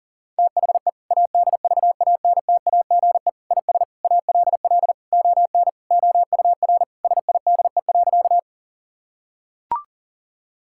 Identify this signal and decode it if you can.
Morse code, heard as a keyed tone, about 40 wpm, 700 Hz. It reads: THE ADVANTAGE IS ALL ON OUR SIDE.
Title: Sentences from Top 1000 Words - Review Part 6 - 40wpm